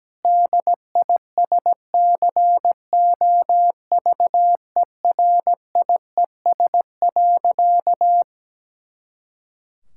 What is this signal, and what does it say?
Morse code, heard as a keyed tone, about 17 wpm, 700 Hz. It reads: DISCOVERIES.
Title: Pangram Challenge v2 - 17wpm